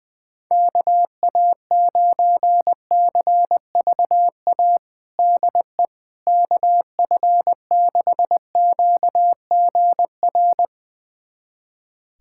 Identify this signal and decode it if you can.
Morse code, heard as a keyed tone, about 20 wpm, 700 Hz. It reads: KA9CVA DE KF6QGR
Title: USA Callsign DE Callsign 20wpm